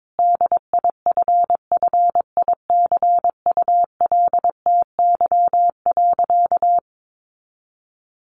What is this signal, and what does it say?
Morse code, heard as a keyed tone, about 22 wpm, 700 Hz. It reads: DIFFICULTY.
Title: Sentences from Top 1500 Words 22wpm